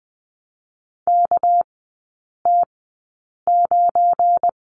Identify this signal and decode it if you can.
Morse code, heard as a keyed tone, about 20 wpm, 700 Hz. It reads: K T 9
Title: 212 Lesson 34e - Number 8 - Three Characters 20wpm